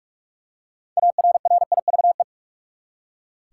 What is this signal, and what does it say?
Morse code, heard as a keyed tone, about 46 wpm, 700 Hz. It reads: ARRIVE
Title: Six Letter Words - 45wpm